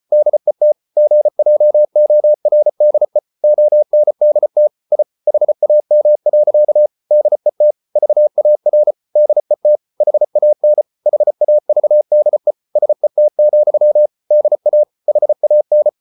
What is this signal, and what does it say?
Morse code, heard as a keyed tone, about 34 wpm, 600 Hz. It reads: DET GJORDE ONDT I HAM. DET VAR DET HAN HAVDE SET, DA HAN
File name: Ferie QRQ 34wpm